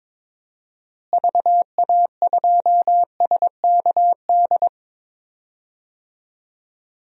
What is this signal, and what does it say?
Morse code, heard as a keyed tone, about 21 wpm, 700 Hz. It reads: VA2SKD
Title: DXCC-100 Call Signs - 6 Characters - Repeated 2x - 22wpm